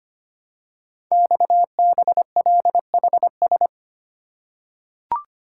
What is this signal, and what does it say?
Morse code, heard as a keyed tone, about 25 wpm, 700 Hz. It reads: XBLHS
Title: Code Groups - 5 Letters - 25wpm